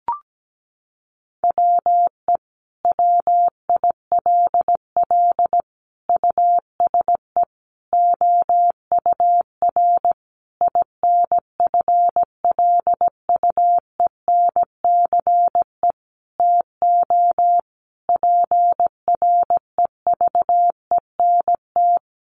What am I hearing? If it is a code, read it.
Morse code, heard as a keyed tone, about 17 wpm, 700 Hz. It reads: WE WILL USE OUR INFLUENCE TO PREVENT
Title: Sentences from Top 1100 Words 17wpm